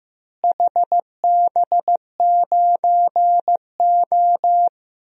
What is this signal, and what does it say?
Morse code, heard as a keyed tone, about 15 wpm, 700 Hz. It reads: HB9O
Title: DXCC-100 Call Signs - 4 Characters - Repeated 2x - 15wpm